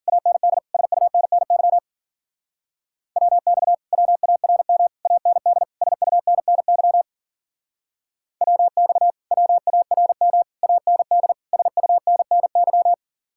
Morse code, heard as a keyed tone, about 52 wpm, 700 Hz. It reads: AND SUNNY WX WARM AND SUNNY WX WARM AND SUNNY